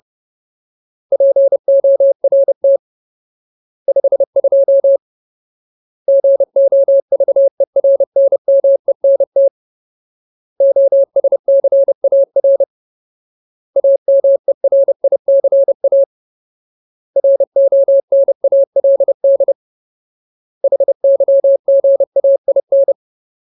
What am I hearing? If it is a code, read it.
Morse code, heard as a keyed tone, about 30 wpm, 550 Hz. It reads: PORT 52 GOVERNMENT OSCAR AMERICA RONALD HYGAIN
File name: Ultimate Word List / 30WPM Ultimate